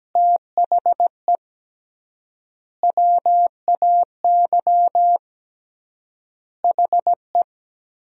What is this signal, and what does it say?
Morse code, heard as a keyed tone, about 17 wpm, 700 Hz. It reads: THE WAY HE
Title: 3x Word-Spacing - Sentences from Top 200 Words - 17wpm